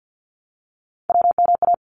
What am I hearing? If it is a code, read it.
Morse code, heard as a keyed tone, about 50 wpm, 700 Hz. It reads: WMA
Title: ARRL Sections 50wpm